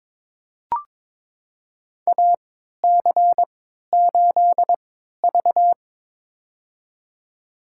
Morse code, heard as a keyed tone, about 23 wpm, spaced as for 13 wpm, 700 Hz. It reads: AC8V
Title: USA Call Signs - 3x Character Spacing - 22wpm